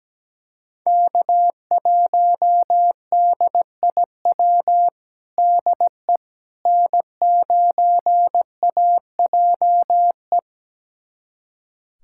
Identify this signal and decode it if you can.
Morse code, heard as a keyed tone, about 17 wpm, 700 Hz. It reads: K1DIW DE N9AJE
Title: USA Callsign DE Callsign - 5 Characters - 17wpm